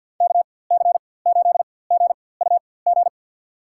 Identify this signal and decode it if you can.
Morse code, heard as a keyed tone, about 51 wpm, spaced as for 23 wpm, 700 Hz. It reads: KC8GUG
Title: USA Call Signs - 6 Characters - 4x Character Spacing - 50wpm